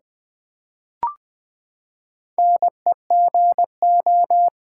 Morse code, heard as a keyed tone, about 20 wpm, 700 Hz. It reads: NEGO
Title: Build A Word - 4 Letters At A Time - 11-Letter Words - 20wpm